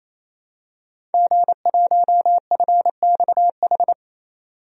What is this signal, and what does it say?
Morse code, heard as a keyed tone, about 28 wpm, 700 Hz. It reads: G1FXH
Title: DXCC-100 Call Signs - 5 Characters - 28wpm